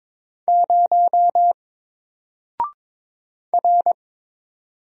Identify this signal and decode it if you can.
Morse code, heard as a keyed tone, about 22 wpm, 700 Hz. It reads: 0 R
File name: Single-Letter-Number-22wpm